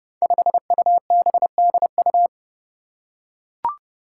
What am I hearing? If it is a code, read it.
Morse code, heard as a keyed tone, about 30 wpm, 700 Hz. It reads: 5UBDU
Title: Five Challenging Letter-Number Combinations 30wpm